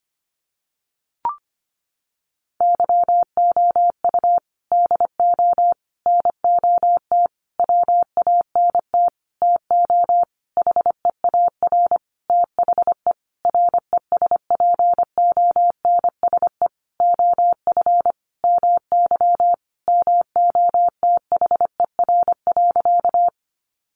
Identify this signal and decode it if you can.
Morse code, heard as a keyed tone, about 25 wpm, 700 Hz. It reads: YOU DO NOT WANT TO HEAR THE RESPONSE OF MY MOTHER.
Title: Sentences from Top 1400 Words 25wpm